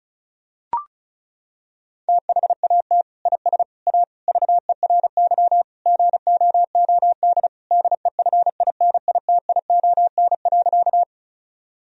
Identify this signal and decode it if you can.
Morse code, heard as a keyed tone, about 36 wpm, 700 Hz. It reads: THAT IS A VERY GOOD DEFINITION.
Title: Sentences from Top 1000 Words - Review Part 4 - 35wpm